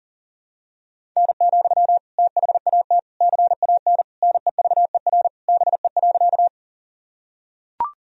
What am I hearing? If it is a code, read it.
Morse code, heard as a keyed tone, about 40 wpm, 700 Hz. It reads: N, THAT CAN NEVER BE.
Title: Sentences from Top 300 Words and Abbreviations 40wpm